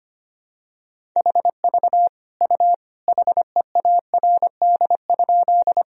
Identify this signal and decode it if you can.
Morse code, heard as a keyed tone, about 25 wpm, 700 Hz. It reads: HV U HEARD?
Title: Sentences from Top 400 Words and Abbreviations 25wpm